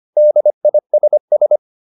Morse code, heard as a keyed tone, about 25 wpm, 600 Hz. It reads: DISS